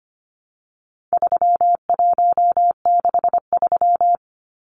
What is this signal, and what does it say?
Morse code, heard as a keyed tone, about 25 wpm, 700 Hz. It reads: 3163
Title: Numbers - 4-Digit 25wpm